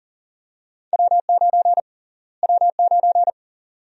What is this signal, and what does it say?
Morse code, heard as a keyed tone, about 40 wpm, 700 Hz. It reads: W9 W9
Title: USA Call Sign Prefixes - Repeated 2x - 40wpm